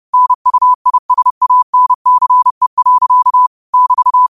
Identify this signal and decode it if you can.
Morse code, heard as a keyed tone, about 30 wpm, 1000 Hz. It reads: NUISANCE. <BT>
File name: qotdfiles / 30wpm